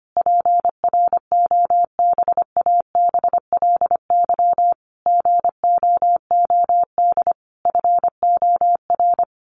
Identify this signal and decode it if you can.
Morse code, heard as a keyed tone, about 25 wpm, 700 Hz. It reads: PROBABLY GOOD FOR
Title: Sentences from Top 1400 Words 25wpm